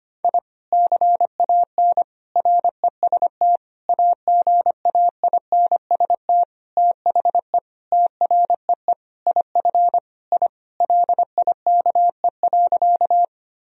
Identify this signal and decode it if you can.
Morse code, heard as a keyed tone, about 25 wpm, 700 Hz. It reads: I CAN REST AGAINST THE TREE IF I LIKE.
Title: Sentences from Top 1000 Words - Review Part 1 - 25wpm